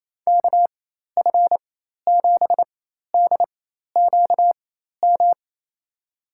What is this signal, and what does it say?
Morse code, heard as a keyed tone, about 27 wpm, spaced as for 13 wpm, 700 Hz. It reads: KF7DQM
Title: USA Call Signs - 6 Characters - 4x Character Spacing - 28wpm